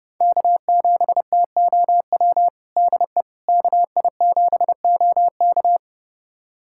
Morse code, heard as a keyed tone, about 30 wpm, 700 Hz. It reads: K7TOW DE KI7OK
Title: USA Callsign DE Callsign - 5 Characters - 30wpm